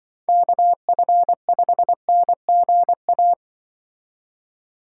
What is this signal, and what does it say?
Morse code, heard as a keyed tone, about 24 wpm, 700 Hz. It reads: KF5NGA
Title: Speed Racing - USA 2x3 Call Signs 20wpm